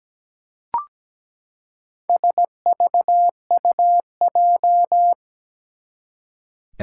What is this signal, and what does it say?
Morse code, heard as a keyed tone, about 17 wpm, 700 Hz. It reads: SVUJ
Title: Four Challenging Letter-Number Combinations 17wpm